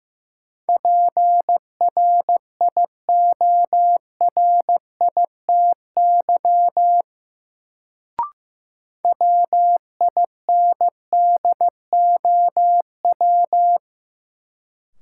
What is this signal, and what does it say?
Morse code, heard as a keyed tone, about 15 wpm, 700 Hz. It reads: PRIORITY WINDOW
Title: Sentences from Top 1600 Words - New Word Drill - 15wpm